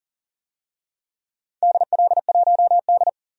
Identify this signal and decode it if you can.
Morse code, heard as a keyed tone, about 40 wpm, 700 Hz. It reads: DL1D
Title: DXCC-100 Call Signs - 4 Characters - 40wpm